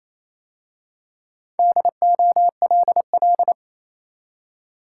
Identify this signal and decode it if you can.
Morse code, heard as a keyed tone, about 27 wpm, 690 Hz. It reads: DOLL
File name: Easy Words - Speed-Racing with Hints - 22wpm